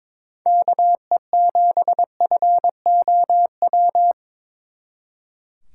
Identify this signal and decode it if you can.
Morse code, heard as a keyed tone, about 26 wpm, 700 Hz. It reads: KE7FOW
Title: Speed Racing - USA 2x3 Call Signs 22wpm